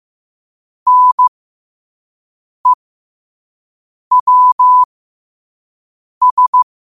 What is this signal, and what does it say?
Morse code, heard as a keyed tone, about 15 wpm, spaced as for 5 wpm, 1000 Hz. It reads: NEWS
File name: qotdfiles / 5wpm